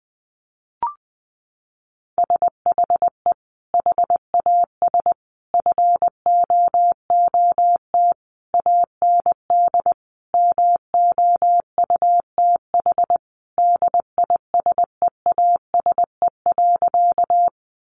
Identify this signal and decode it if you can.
Morse code, heard as a keyed tone, about 20 wpm, 700 Hz. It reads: SHE HAS FOOT AND MOUTH DISEASE.
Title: Sentences from Top 1500 Words 20wpm